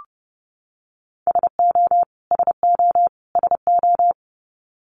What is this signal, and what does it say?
Morse code, heard as a keyed tone, about 30 wpm, 700 Hz. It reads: SO SO SO
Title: Top 500 Words Repeated 30wpm